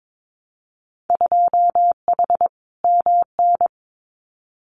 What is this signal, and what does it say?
Morse code, heard as a keyed tone, about 22 wpm, 700 Hz. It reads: 2H MN
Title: Winter Field Day Exchanges 22wpm